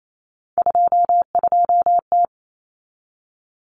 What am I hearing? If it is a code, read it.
Morse code, heard as a keyed tone, about 28 wpm, 700 Hz. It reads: 22T
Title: Common Cut Numbers - 3-Digit 28wpm